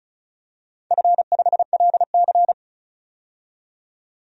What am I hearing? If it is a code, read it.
Morse code, heard as a keyed tone, about 36 wpm, 700 Hz. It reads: F5LC
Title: DXCC-100 Call Signs - 4 Characters - 35wpm